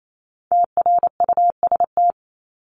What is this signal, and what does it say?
Morse code, heard as a keyed tone, about 28 wpm, 700 Hz. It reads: TRUST
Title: Top 1751-2000 Words 28wpm